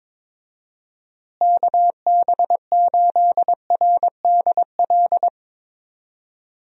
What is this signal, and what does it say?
Morse code, heard as a keyed tone, about 22 wpm, 700 Hz. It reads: KB8RDL
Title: USA Call Signs - 6 Characters - Repeated 2x - 22wpm